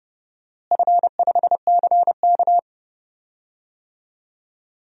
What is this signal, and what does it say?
Morse code, heard as a keyed tone, about 32 wpm, 700 Hz. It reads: F5CK
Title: DXCC-100 Call Signs - 4 Characters - Repeated 3x - 30wpm